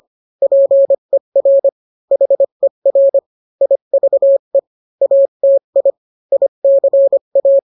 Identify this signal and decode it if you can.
Morse code, heard as a keyed tone, about 25 wpm, 550 Hz. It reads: PER HER IVE ATI ICA